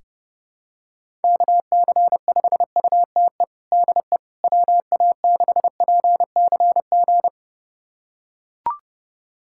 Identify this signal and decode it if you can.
Morse code, heard as a keyed tone, about 30 wpm, 700 Hz. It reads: KC5UTE DE WA6PCG
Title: USA Callsign DE Callsign 30wpm